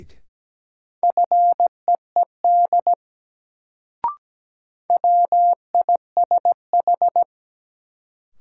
Morse code, heard as a keyed tone, about 17 wpm, 700 Hz. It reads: FEED WISH
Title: Sentences from Top 1600 Words - New Word Drill Alt - 17wpm